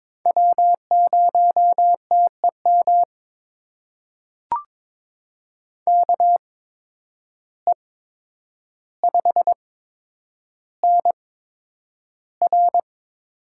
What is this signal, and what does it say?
Morse code, heard as a keyed tone, about 22 wpm, spaced as for 9 wpm, 700 Hz. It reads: W0TEM KE5NR
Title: USA Call Signs - 8x Character Spacing - 22wpm